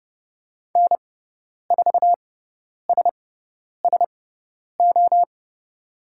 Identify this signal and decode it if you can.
Morse code, heard as a keyed tone, about 30 wpm, spaced as for 10 wpm, 700 Hz. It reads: N4SSO
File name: Lesson 18g - Letter m - Call Signs 30wpm